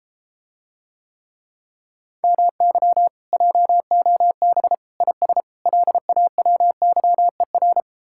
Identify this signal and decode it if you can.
Morse code, heard as a keyed tone, about 33 wpm, 700 Hz. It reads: MY JOB IS LAWYER